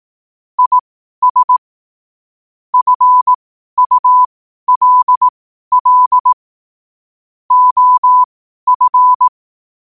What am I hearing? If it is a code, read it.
Morse code, heard as a keyed tone, about 18 wpm, spaced as for 12 wpm, 1000 Hz. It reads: IS FULL OF
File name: qotdfiles / 12wpm